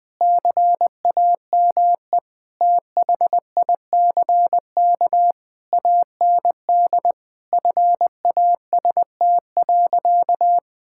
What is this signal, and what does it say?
Morse code, heard as a keyed tone, about 21 wpm, 700 Hz. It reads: CAME THICK AND FAST.
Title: Sentences from Top 1600 Words 20wpm